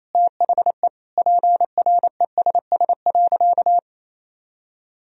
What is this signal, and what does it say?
Morse code, heard as a keyed tone, about 28 wpm, 700 Hz. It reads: THE PRESS.
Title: Sentences from Top 1000 Words - Review Part 1 - 28wpm